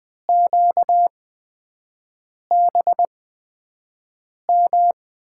Morse code, heard as a keyed tone, about 20 wpm, spaced as for 5 wpm, 700 Hz. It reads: QBM
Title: USA Call Signs - 5 Characters - 8x Character Spacing - 20wpm